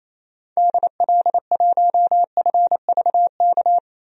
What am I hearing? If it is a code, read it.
Morse code, heard as a keyed tone, about 28 wpm, 700 Hz. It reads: DL1FVK
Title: DXCC-100 Call Signs - 6 Characters - 28wpm